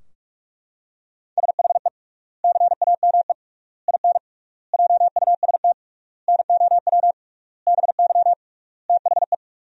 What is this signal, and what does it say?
Morse code, heard as a keyed tone, about 42 wpm, 700 Hz. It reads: SHE CAME IN JUST NOW BY THE
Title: 3x Word-Spacing - Sentences from Top 300 Words - 45wpm